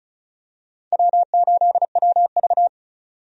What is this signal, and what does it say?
Morse code, heard as a keyed tone, about 34 wpm, 700 Hz. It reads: W8WV
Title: USA Call Signs - 4 Characters - 35wpm